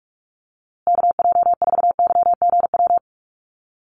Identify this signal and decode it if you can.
Morse code, heard as a keyed tone, about 44 wpm, 700 Hz. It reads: KJ4YGW
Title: USA Call Signs Repeated 3x 45wpm